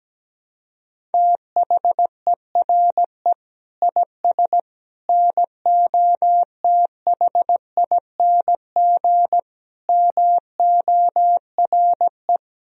Morse code, heard as a keyed tone, about 17 wpm, 700 Hz. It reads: THERE IS NOTHING MORE
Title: Sentences from Top 500 Words and Abbreviations 17wpm